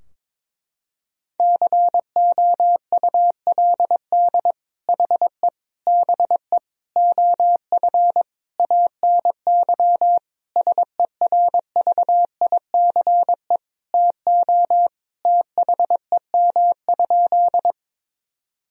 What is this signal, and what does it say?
Morse code, heard as a keyed tone, about 22 wpm, 700 Hz. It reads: COULD HE BE OF ANY SERVICE TO THEM?
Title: Sentences from Top 1000 Words - Review Part 4 - 22wpm